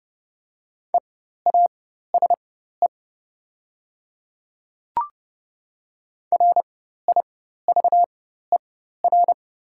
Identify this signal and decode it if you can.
Morse code, heard as a keyed tone, about 30 wpm, spaced as for 14 wpm, 700 Hz. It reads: EASE RIVER
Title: 4x Character Spacing - Top 300 Words - 30wpm